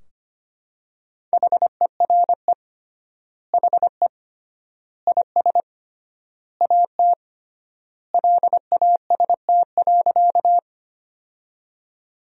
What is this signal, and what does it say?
Morse code, heard as a keyed tone, about 25 wpm, 700 Hz. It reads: HERE HE IS AT LAST.
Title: 3x Word-Spacing - Sentences from Top 300 Words - 25wpm